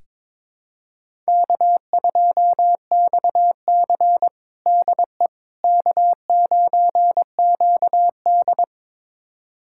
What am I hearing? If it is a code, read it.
Morse code, heard as a keyed tone, about 22 wpm, 700 Hz. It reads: K2XC DE K9QD
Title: USA Callsign DE Callsign - 4 Characters - 22wpm